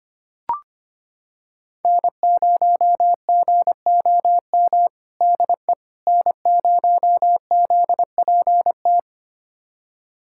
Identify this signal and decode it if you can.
Morse code, heard as a keyed tone, about 25 wpm, 700 Hz. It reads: N0GOM DE N0ZPT
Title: USA Callsign DE Callsign - 5 Characters - 25wpm